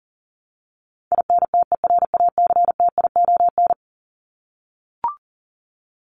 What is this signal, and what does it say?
Morse code, heard as a keyed tone, about 40 wpm, 700 Hz. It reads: INTERACTION
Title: Top 1251-1500 Words 40wpm